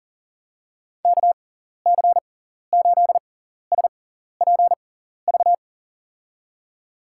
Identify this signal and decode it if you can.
Morse code, heard as a keyed tone, about 40 wpm, spaced as for 14 wpm, 700 Hz. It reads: KC8SPV
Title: USA Call Signs - 6x Character Spacing - 40wpm